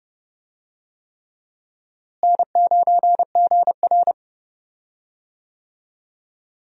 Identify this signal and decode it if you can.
Morse code, heard as a keyed tone, about 30 wpm, 700 Hz. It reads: N9GR
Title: USA Call Signs - 4 Characters - Repeated 3x - 30wpm